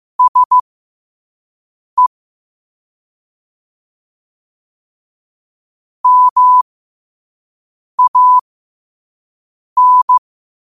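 Morse code, heard as a keyed tone, about 15 wpm, spaced as for 5 wpm, 1000 Hz. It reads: SE MAN
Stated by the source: Morse code at 5 wpm.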